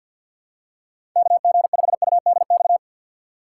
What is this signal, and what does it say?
Morse code, heard as a keyed tone, about 52 wpm, 700 Hz. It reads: KG5UDX
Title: USA Call Signs - 6 Characters - 6x Character Spacing - 50wpm